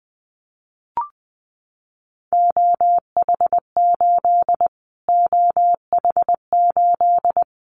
Morse code, heard as a keyed tone, about 19 wpm, 700 Hz. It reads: OH8 OH8
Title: Callsign Prefixes for Top 25 DXCC - Repeated 3x - 20wpm